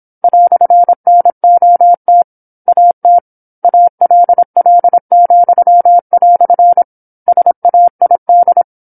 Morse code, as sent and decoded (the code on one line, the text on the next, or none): .-..-. -. --- - / .- - / .- .-.. .-.. --..-- .-..-. / ... .- .. -..
"NOT AT ALL," SAID